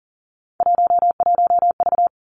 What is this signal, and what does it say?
Morse code, heard as a keyed tone, about 40 wpm, 700 Hz. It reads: 11V
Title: Three Challenging Letter-Number Combinations 40wpm